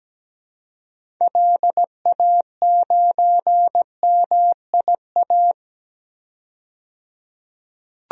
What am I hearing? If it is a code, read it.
Morse code, heard as a keyed tone, about 17 wpm, 700 Hz. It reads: LA9MIA
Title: DXCC-100 Call Signs - 6 Characters - 17wpm